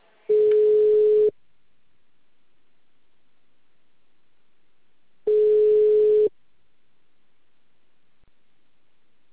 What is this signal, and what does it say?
Ringback tone (Europe, 425 Hz).